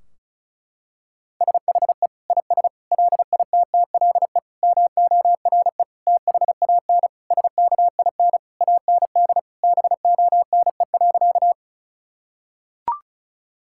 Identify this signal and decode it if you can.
Morse code, heard as a keyed tone, about 34 wpm, 700 Hz. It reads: SHE IS LITTLE MORE THAN SKIN AND BONE.
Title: Sentences from Top 1600 Words 35wpm